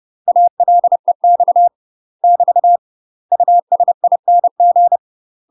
Morse code, heard as a keyed tone, about 30 wpm, 700 Hz. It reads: ALEX <BT> USING
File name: UK MASTER QSOs TRAINING / 30WPM / UK MASTER QSOs 0013